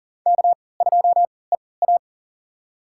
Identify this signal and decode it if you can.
Morse code, heard as a keyed tone, about 39 wpm, spaced as for 22 wpm, 700 Hz. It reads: K2EA
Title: USA Call Signs - 4 Characters - 3x Character Spacing - 40wpm